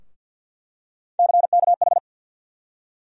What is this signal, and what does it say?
Morse code, heard as a keyed tone, about 48 wpm, 700 Hz. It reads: CKH
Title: USA Call Sign Postfixes - 3 Letters 50wpm